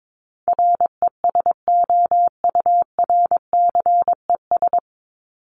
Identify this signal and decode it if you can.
Morse code, heard as a keyed tone, about 22 wpm, 700 Hz. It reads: RESOURCES
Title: ES:S Suffix - Top 100 Words with Base Words 22wpm